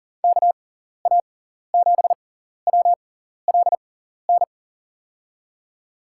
Morse code, heard as a keyed tone, about 40 wpm, spaced as for 14 wpm, 700 Hz. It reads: KA7WLN
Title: USA Call Signs - 6x Character Spacing - 40wpm